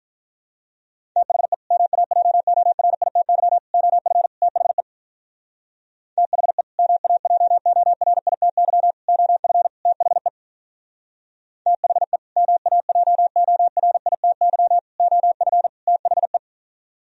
Morse code, heard as a keyed tone, about 43 wpm, 700 Hz. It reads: THE MAJORITY OF THE THE MAJORITY OF THE THE MAJORITY OF THE
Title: Speed-Racing - Sets of 4 Words 35wpm